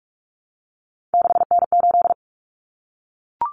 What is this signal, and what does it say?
Morse code, heard as a keyed tone, about 44 wpm, 700 Hz. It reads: 6N8
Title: Common Cut Numbers - 3-Digit 45wpm